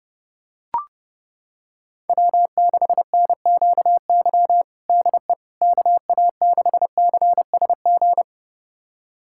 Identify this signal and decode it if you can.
Morse code, heard as a keyed tone, about 30 wpm, 700 Hz. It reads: W6NQY DE KA6CSG
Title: USA Callsign DE Callsign 30wpm